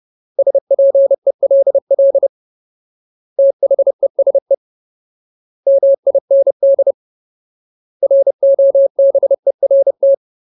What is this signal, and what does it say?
Morse code, heard as a keyed tone, about 30 wpm, 550 Hz. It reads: SPELL THESE MIND ROBERT
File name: Ultimate Word List / 30WPM Ultimate